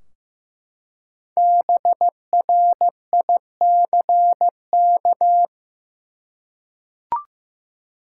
Morse code, heard as a keyed tone, about 15 wpm, 700 Hz. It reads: BRICK
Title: Five Letter Words - 15wpm